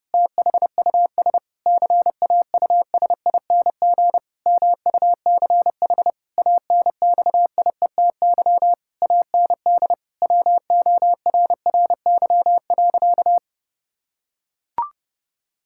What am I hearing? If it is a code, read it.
Morse code, heard as a keyed tone, about 30 wpm, 700 Hz. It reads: THUS CAUSING MUCH ANXIETY AND WORRY.
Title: Pangram Challenge v2 - 30wpm